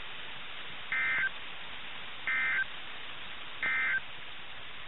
Emergency Alert System (SAME) tones, heard as harsh buzzing bursts. It is heard from 0.8 to 4.1 s.
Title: NOAA Weather Radio - KIH61